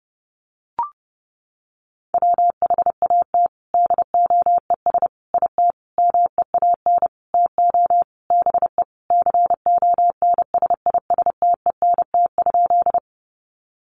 Morse code, heard as a keyed tone, about 30 wpm, 700 Hz. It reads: WHAT DOES IT MEAN TO BE CONSISTENT?
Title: Sentences from Top 1500 Words 30wpm